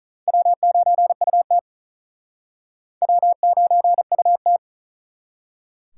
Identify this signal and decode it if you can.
Morse code, heard as a keyed tone, about 48 wpm, 700 Hz. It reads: W9UT W9UT